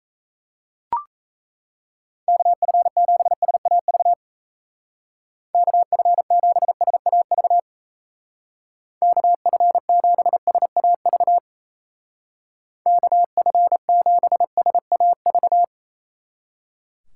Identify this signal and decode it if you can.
Morse code, heard as a keyed tone, about 33 wpm, 700 Hz. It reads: KF7SAV KF7SAV KF7SAV KF7SAV
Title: Speed Racing - USA 2x3 Call Signs 28wpm